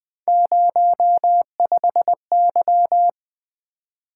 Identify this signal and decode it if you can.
Morse code, heard as a keyed tone, about 20 wpm, 700 Hz. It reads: 05Y
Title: Three Letter-Number Combinations - 20wpm